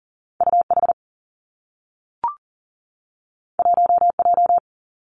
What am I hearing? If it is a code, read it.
Morse code, heard as a keyed tone, about 39 wpm, 700 Hz. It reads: UH 1J
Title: Two Challenging Letter-Number Combinations 40wpm